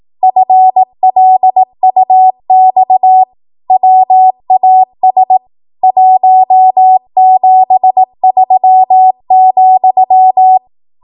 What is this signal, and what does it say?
Morse code, heard as a keyed tone, about 18 wpm, 750 Hz. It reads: FLUX WAS 173,
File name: Archive / 18 WPM / 18WPM